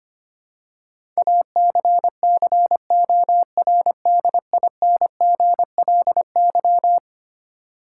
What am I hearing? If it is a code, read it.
Morse code, heard as a keyed tone, about 25 wpm, 700 Hz. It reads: ACCORDINGLY
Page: morse-code-ninja-practice-25wpm